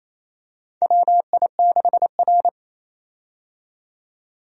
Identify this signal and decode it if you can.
Morse code, heard as a keyed tone, about 28 wpm, 700 Hz. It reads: WI6R
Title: USA Call Signs - 4 Characters - 28wpm